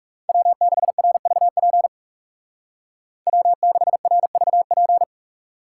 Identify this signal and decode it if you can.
Morse code, heard as a keyed tone, about 37 wpm, 700 Hz. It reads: W6RVP W6RVP
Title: Speed Racing - USA Call Signs 30wpm